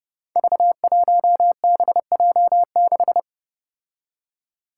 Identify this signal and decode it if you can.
Morse code, heard as a keyed tone, about 30 wpm, 700 Hz. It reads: V1BJ6
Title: Five Challenging Letter-Number Combinations 30wpm